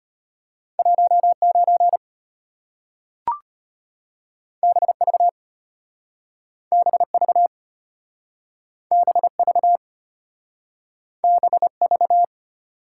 Morse code, heard as a keyed tone, about 34 wpm, 700 Hz. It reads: 19 BV BV BV BV